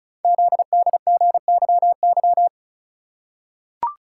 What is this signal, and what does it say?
Morse code, heard as a keyed tone, about 36 wpm, 700 Hz. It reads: ZDGYY
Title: Code Groups - 5 Letters - 35wpm